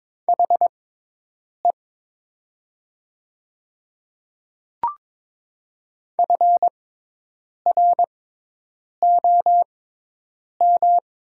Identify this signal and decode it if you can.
Morse code, heard as a keyed tone, about 22 wpm, spaced as for 8 wpm, 700 Hz. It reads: HE FROM
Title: 6x Character Spacing - Top 100 Words - 22wpm